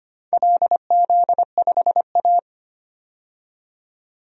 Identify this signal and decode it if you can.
Morse code, heard as a keyed tone, about 25 wpm, 700 Hz. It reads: LZ5A
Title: DXCC-100 Call Signs - 4 Characters - 25wpm